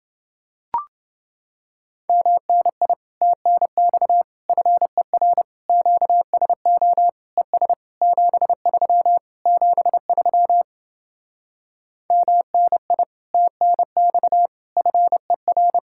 Morse code, heard as a keyed tone, about 30 wpm, 700 Hz. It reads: MNI TNX FER QSO ES 73 73 MNI TNX FER